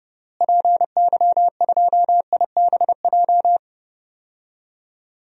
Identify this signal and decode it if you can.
Morse code, heard as a keyed tone, about 30 wpm, 700 Hz. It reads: PY2IBJ
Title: DXCC-100 Call Signs - 6 Characters - Repeated 3x - 30wpm